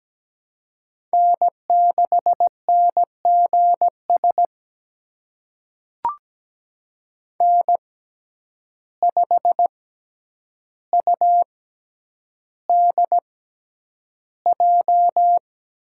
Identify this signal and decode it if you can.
Morse code, heard as a keyed tone, about 17 wpm, spaced as for 9 wpm, 700 Hz. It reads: N6NGS N5UDJ
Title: USA Call Signs - 6x Character Spacing - 17wpm